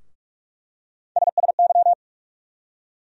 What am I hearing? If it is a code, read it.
Morse code, heard as a keyed tone, about 47 wpm, 700 Hz. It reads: SSY
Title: USA Call Sign Postfixes - 3 Letters 45wpm